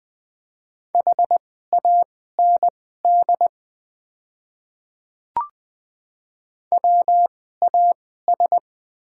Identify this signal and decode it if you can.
Morse code, heard as a keyed tone, about 20 wpm, spaced as for 15 wpm, 700 Hz. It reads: HAND WAS
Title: 2x Character Spacing - Top 500 Words - 20wpm